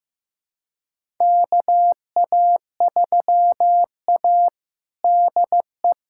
Morse code, heard as a keyed tone, about 15 wpm, 700 Hz. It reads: KA3A DE
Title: USA Callsign DE Callsign - 4 Characters - 15wpm